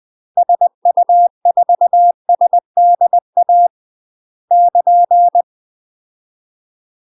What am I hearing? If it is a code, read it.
Morse code, heard as a keyed tone, about 20 wpm, 700 Hz. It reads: SU4SDA <KN>
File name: UK MASTER QSOs TRAINING / 20WPM / UK MASTER QSOs 0060